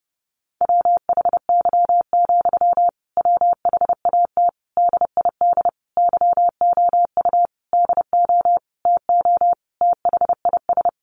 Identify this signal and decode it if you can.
Morse code, heard as a keyed tone, about 30 wpm, 700 Hz. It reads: WHY, WHAT DID YOU DO TO THIS
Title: Sentences from Top 1500 Words 30wpm